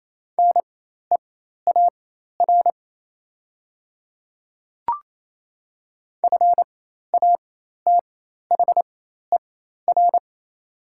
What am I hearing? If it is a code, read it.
Morse code, heard as a keyed tone, about 28 wpm, spaced as for 13 wpm, 690 Hz. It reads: NEAR FATHER
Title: 4x Character Spacing - Top 500 Words - 28wpm